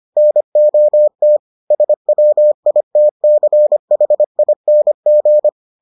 Morse code, heard as a keyed tone, about 25 wpm, 600 Hz. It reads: NOT SWITCHING